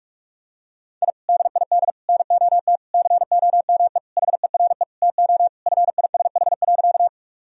Morse code, heard as a keyed tone, about 42 wpm, 700 Hz. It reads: I DID NOT COME HERE TO FISH.
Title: Sentences from Top 1000 Words - Review Part 1 - 45wpm